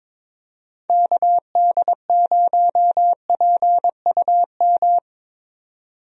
Morse code, heard as a keyed tone, about 22 wpm, 700 Hz. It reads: KD0PUM
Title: USA Call Signs - 6 Characters - Repeated 3x - 22wpm